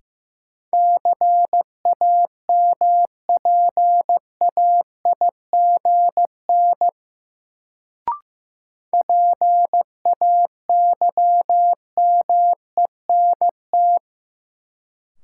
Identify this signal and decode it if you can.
Morse code, heard as a keyed tone, about 15 wpm, 700 Hz. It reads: CAMPAIGN PAYMENT
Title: Sentences from Top 1200 Words - New Word Drill - 15wpm